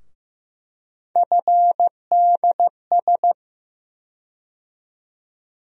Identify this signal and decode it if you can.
Morse code, heard as a keyed tone, about 15 wpm, 700 Hz. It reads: FDS
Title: USA Call Sign Postfixes - 3 Letters 15wpm